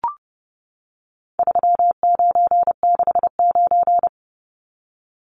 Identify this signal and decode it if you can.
Morse code, heard as a keyed tone, about 30 wpm, 700 Hz. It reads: 3969
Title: Numbers - 4-Digit 30wpm